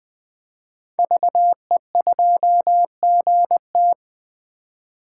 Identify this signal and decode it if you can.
Morse code, heard as a keyed tone, about 20 wpm, 700 Hz. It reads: VE2GT
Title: 2023 SOTA Chaser Call Signs - 20wpm